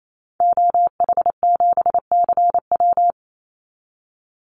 Morse code, heard as a keyed tone, about 28 wpm, 700 Hz. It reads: OH7CW
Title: Call Signs Repeated 2x - 28wpm